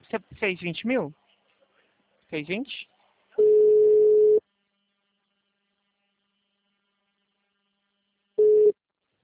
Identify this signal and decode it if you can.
Ringback tone (Europe, 425 Hz), with speech.